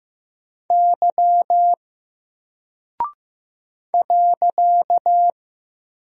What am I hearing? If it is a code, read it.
Morse code, heard as a keyed tone, about 15 wpm, 700 Hz. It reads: Y .
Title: Single Character (Letter:Number:Punctuation:Prosign) Alt 15wpm